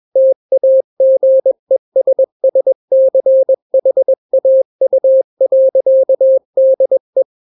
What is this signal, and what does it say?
Morse code, heard as a keyed tone, about 21 wpm, 530 Hz. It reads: TAGESSCHAU.DE
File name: daily / 20wpm